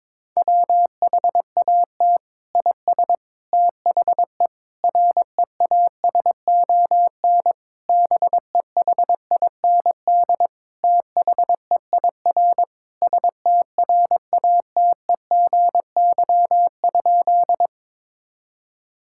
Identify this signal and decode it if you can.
Morse code, heard as a keyed tone, about 22 wpm, 700 Hz. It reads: WHAT IS THE REASON BEHIND THEIR STRATEGY?
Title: Sentences from Top 1000 Words - Review Part 5 - 22wpm